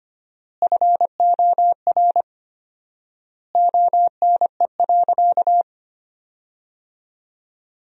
Morse code, heard as a keyed tone, about 25 wpm, 700 Hz. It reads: FOR ONE.
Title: 4x Word-Spacing - Sentences from Top 300 Words - 25wpm